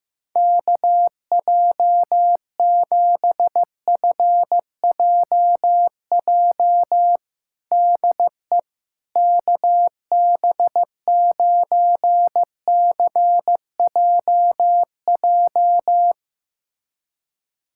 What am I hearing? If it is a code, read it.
Morse code, heard as a keyed tone, about 15 wpm, 700 Hz. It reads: KJ7FJJ DE KB9CJJ
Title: USA Callsign DE Callsign - 6 Characters - 15wpm